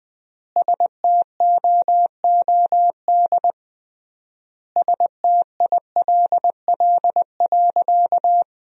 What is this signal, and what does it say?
Morse code, heard as a keyed tone, about 21 wpm, 700 Hz. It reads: STOOD STILL.
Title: 3x Word-Spacing - Sentences from Top 500 Words - 20wpm